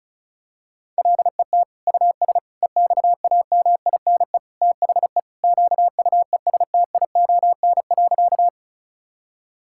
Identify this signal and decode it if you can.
Morse code, heard as a keyed tone, about 36 wpm, 700 Hz. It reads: LET US EXAMINE THE QUESTION.
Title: Sentences from Top 1600 Words 35wpm